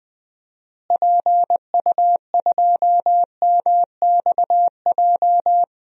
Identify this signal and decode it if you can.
Morse code, heard as a keyed tone, about 20 wpm, 690 Hz. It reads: PU2MXJ
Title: DXCC-100 Call Signs - 6 Characters - Repeated 2x - 20wpm